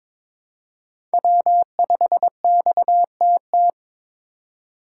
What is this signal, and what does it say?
Morse code, heard as a keyed tone, about 22 wpm, 700 Hz. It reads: W5XTT
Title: USA Call Signs - 5 Characters - 3x Character Spacing - 22wpm